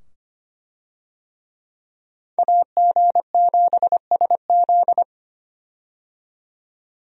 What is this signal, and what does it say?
Morse code, heard as a keyed tone, about 25 wpm, 700 Hz. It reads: AG7SZ
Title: USA Call Signs - 5 Characters - 25wpm